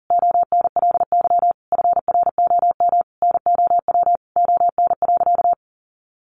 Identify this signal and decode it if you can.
Morse code, heard as a keyed tone, about 40 wpm, 700 Hz. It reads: ONLY FROM NOW ON.
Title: Easy Phrases 40wpm